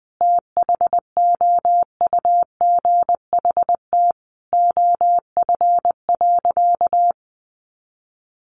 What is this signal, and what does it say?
Morse code, heard as a keyed tone, about 20 wpm, 700 Hz. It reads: THOUGHT OF.
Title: Sentences from Top 300 Words 20wpm